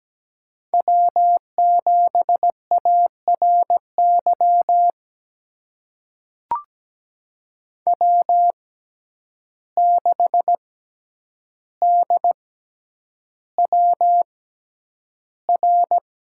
Morse code, heard as a keyed tone, about 17 wpm, spaced as for 9 wpm, 700 Hz. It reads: W7ARY W6DWR
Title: USA Call Signs - 5 Characters - 6x Character Spacing - 17wpm